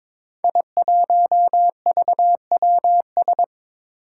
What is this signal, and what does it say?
Morse code, heard as a keyed tone, about 22 wpm, 700 Hz. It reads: I1VWS
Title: Five Challenging Letter-Number Combinations 22wpm